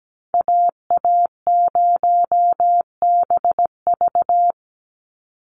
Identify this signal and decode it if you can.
Morse code, heard as a keyed tone, about 17 wpm, 700 Hz. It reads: AA0BV
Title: NA SOTA Activator Callsigns 17wpm